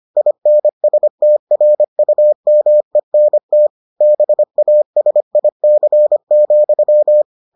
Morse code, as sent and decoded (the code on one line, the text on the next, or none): .. -. ... - .-. ..- -- . -. - / -... .- ... .. -.-. --..--
INSTRUMENT BASIC,